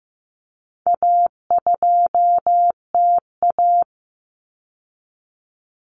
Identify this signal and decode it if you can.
Morse code, heard as a keyed tone, about 15 wpm, 700 Hz. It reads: A2TA
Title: Common Cut Numbers - 4-Digit 15wpm